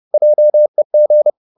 Morse code, heard as a keyed tone, about 30 wpm, 600 Hz. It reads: JEG
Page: Ferie 30 wpm